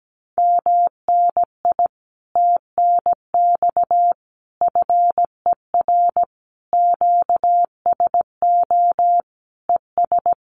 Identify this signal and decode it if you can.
Morse code, heard as a keyed tone, about 17 wpm, 700 Hz. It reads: MNI TNX FER QSO ES
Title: Common QSO Phrases Encore 17wpm